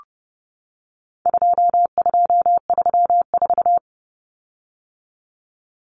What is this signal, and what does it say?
Morse code, heard as a keyed tone, about 30 wpm, 700 Hz. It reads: 2234
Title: Common Cut Numbers - 4-Digit 30wpm